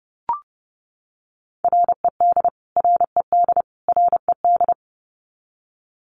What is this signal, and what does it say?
Morse code, heard as a keyed tone, about 30 wpm, 700 Hz. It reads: RED RED RED
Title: Top 500 Words Repeated 30wpm